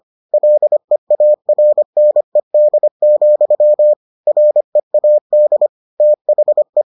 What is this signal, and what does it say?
Morse code, heard as a keyed tone, about 25 wpm, 600 Hz. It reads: LEARNED, READ THE